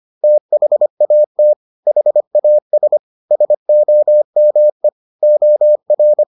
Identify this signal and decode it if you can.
Morse code, heard as a keyed tone, about 25 wpm, 600 Hz. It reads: THAT HAS SOME OR